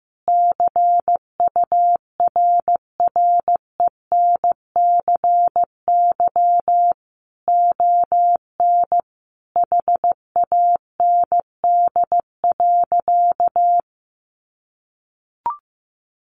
Morse code, heard as a keyed tone, about 15 wpm, 700 Hz. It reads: CURRENCY ON HAND.
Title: Sentences from Top 1500 Words 15wpm